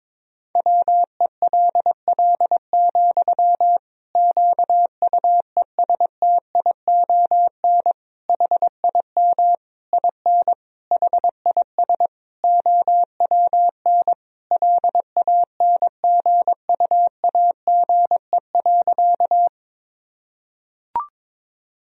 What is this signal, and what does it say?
Morse code, heard as a keyed tone, about 22 wpm, 700 Hz. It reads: WELL, QUESTION HIM IN HIS OWN LANGUAGE.
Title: Sentences from Top 1000 Words - Review Part 3 - 22wpm